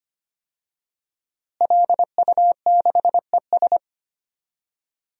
Morse code, heard as a keyed tone, about 25 wpm, 700 Hz. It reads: LU6ES
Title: DXCC-100 Call Signs - 5 Characters - Repeated 2x - 25wpm